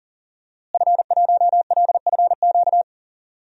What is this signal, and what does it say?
Morse code, heard as a keyed tone, about 40 wpm, 700 Hz. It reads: F1LFQ
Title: DXCC-100 Call Signs - 5 Characters - Repeated 3x - 40wpm